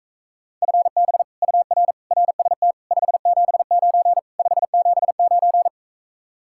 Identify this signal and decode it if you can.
Morse code, heard as a keyed tone, about 45 wpm, 700 Hz. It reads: FB UR RST 579 579